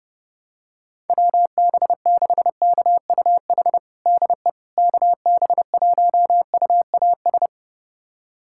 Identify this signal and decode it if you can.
Morse code, heard as a keyed tone, about 30 wpm, 700 Hz. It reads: WB6KUH DE KB1UAS
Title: USA Callsign DE Callsign - 6 Characters - 30wpm